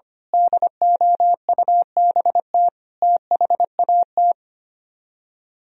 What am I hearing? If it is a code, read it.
Morse code, heard as a keyed tone, about 24 wpm, 700 Hz. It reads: DOUBT THAT
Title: Speed-Racing - Sets of 5 Words 25wpm